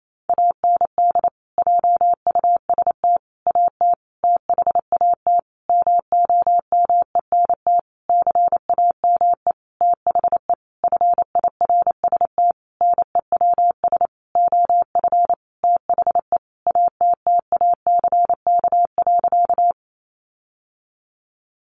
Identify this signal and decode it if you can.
Morse code, heard as a keyed tone, about 28 wpm, 700 Hz. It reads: AND JUST AT THAT MOMENT CAME THE FIRST NEWS OF THE ATTACK.
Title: Sentences from Top 1500 Words 28wpm